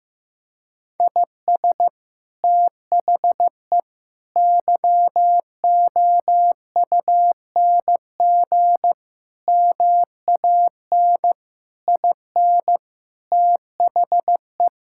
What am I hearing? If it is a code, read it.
Morse code, heard as a keyed tone, about 16 wpm, 700 Hz. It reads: IS THE YOUNG MAN IN THE
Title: Sentences from Top 1600 Words 15wpm